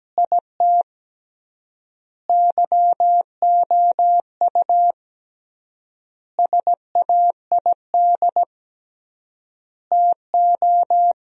Morse code, heard as a keyed tone, about 17 wpm, 700 Hz. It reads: IT YOU SAID TO
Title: 3x Word-Spacing - Sentences from Top 200 Words - 17wpm